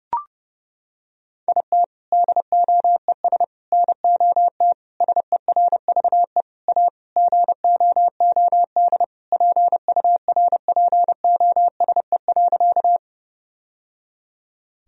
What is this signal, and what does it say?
Morse code, heard as a keyed tone, about 30 wpm, 700 Hz. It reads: IT DOES NOT SERVE A GOOD PURPOSE.
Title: Sentences from Top 1000 Words - Review Part 2 - 30wpm